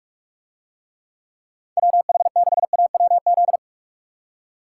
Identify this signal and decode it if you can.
Morse code, heard as a keyed tone, about 44 wpm, 700 Hz. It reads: WH6AWZ